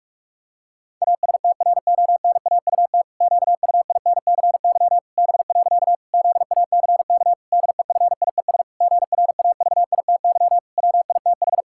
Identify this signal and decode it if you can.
Morse code, heard as a keyed tone, about 46 wpm, 700 Hz. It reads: ASTRONAUT QUINCY B. ZACK DEFIES GRAVITY WITH
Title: Pangram Challenge v2 - 45wpm